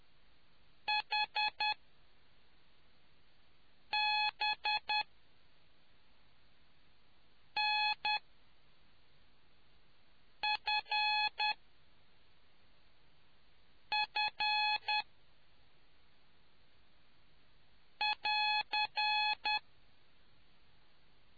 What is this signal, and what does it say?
Morse code, heard as a keyed tone, about 10 wpm, spaced as for 3 wpm, 850 Hz. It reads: HBNFF <AR>